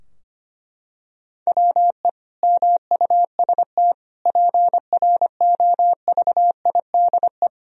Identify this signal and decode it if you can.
Morse code, heard as a keyed tone, about 25 wpm, 700 Hz. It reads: WE MUST PROVIDE
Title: Sentences from Top 1000 Words - Review Part 4 - 25wpm